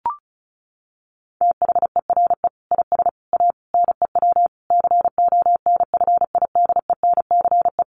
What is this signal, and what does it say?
Morse code, heard as a keyed tone, about 34 wpm, 700 Hz. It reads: THERE IS A NEW CONFIDENCE
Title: Sentences from Top 1500 Words 35wpm